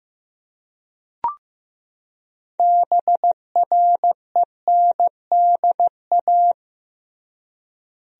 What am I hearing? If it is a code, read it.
Morse code, heard as a keyed tone, about 15 wpm, 700 Hz. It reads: BRENDA
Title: USA Common Names - 15wpm